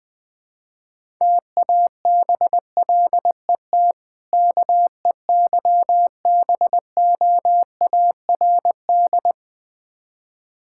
Morse code, heard as a keyed tone, about 21 wpm, 700 Hz. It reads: TABLET KEYBOARD
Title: Difficult Words with Hints - 20wpm